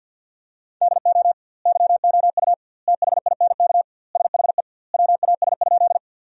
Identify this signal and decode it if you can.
Morse code, heard as a keyed tone, about 50 wpm, 700 Hz. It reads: DO YOU THINK SHE WAS?
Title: 2x Word-Spacing - Sentences from Top 200 Words - 50wpm